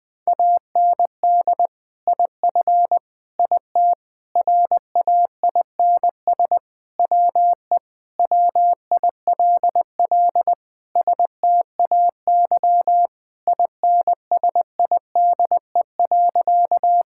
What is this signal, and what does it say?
Morse code, heard as a keyed tone, about 21 wpm, 700 Hz. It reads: AND IF IT RAINS WE WILL STAY INSIDE.
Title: Sentences with 2-Word Phrases - 20wpm